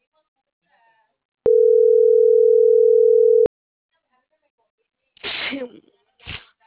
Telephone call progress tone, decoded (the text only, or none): ringback tone